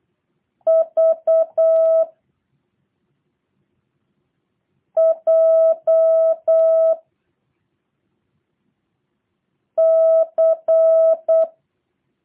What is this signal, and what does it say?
Morse code, heard as a keyed tone, about 8 wpm, 650 Hz. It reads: V J C
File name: jh8cbh / 2023cw / review1-5